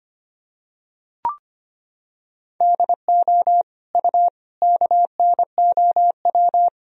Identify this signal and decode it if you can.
Morse code, heard as a keyed tone, about 25 wpm, 700 Hz. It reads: DO U KNOW
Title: Sentences from Top 100 Words and Abbreviations 25wpm